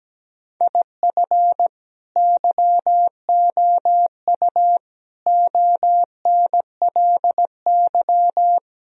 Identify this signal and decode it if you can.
Morse code, heard as a keyed tone, about 17 wpm, 700 Hz. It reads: IF YOU ONLY
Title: Sentences from Top 1000 Words - Review Part 4 - 17wpm